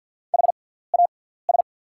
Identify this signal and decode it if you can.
Morse code, heard as a keyed tone, about 48 wpm, spaced as for 17 wpm, 690 Hz. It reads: HAS
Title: 6x Character Spacing - Top 300 Words - 50wpm